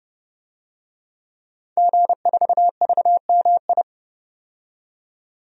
Morse code, heard as a keyed tone, about 30 wpm, 700 Hz. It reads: G4VMI
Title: DXCC-100 Call Signs - 5 Characters - 30wpm